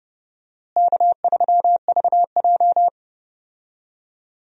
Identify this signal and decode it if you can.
Morse code, heard as a keyed tone, about 31 wpm, 700 Hz. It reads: K3VJ
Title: USA Call Signs - 4 Characters - Repeated 3x - 30wpm